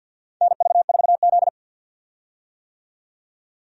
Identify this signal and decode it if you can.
Morse code, heard as a keyed tone, about 53 wpm, 700 Hz. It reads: NV4Z
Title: USA Call Signs - 4 Characters - 50wpm